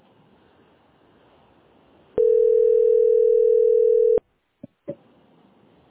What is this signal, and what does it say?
Ringback tone (North America), 440 + 480 Hz.
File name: 799a1f42-e47a-4431-be05